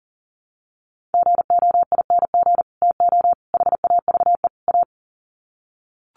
Morse code, heard as a keyed tone, about 40 wpm, 700 Hz. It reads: GOING TO HAVE A
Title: Sets of 4 Words 40wpm